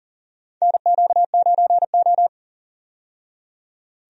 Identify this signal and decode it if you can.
Morse code, heard as a keyed tone, about 40 wpm, 700 Hz. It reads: NQ9O
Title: USA Call Signs - 4 Characters - 40wpm